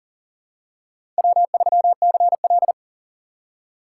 Morse code, heard as a keyed tone, about 40 wpm, 700 Hz. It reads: W3CL
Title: USA Call Signs - 4 Characters - 2x Character Spacing - 40wpm